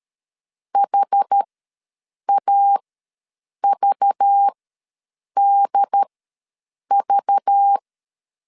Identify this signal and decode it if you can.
Morse code, heard as a keyed tone, about 13 wpm, spaced as for 7 wpm, 800 Hz. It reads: HAVDV
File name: LEZIONI / 5 LBVC / 5 LBV ws 12wpm 3015